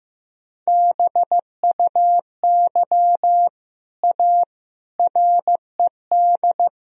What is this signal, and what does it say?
Morse code, heard as a keyed tone, about 15 wpm, 700 Hz. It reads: BUY A RED
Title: Sentences from Top 1000 Words - Review Part 2 - 15wpm